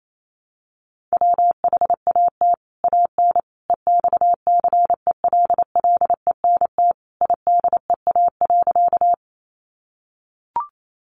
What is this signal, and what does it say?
Morse code, heard as a keyed tone, about 28 wpm, 700 Hz. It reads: WHAT AN EXCELLENT IDEA.
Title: Sentences from Top 1000 Words 28wpm